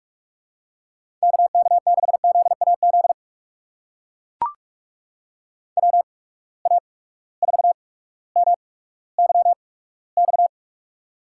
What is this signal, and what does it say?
Morse code, heard as a keyed tone, about 44 wpm, spaced as for 19 wpm, 700 Hz. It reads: KK6ZAZ WA4MYX
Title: USA Call Signs - 6 Characters - 8x Character Spacing - 45wpm